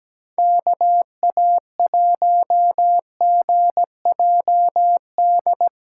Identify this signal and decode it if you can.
Morse code, heard as a keyed tone, about 17 wpm, 700 Hz. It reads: KA1GJD
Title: USA Callsign DE Callsign - 6 Characters - 17wpm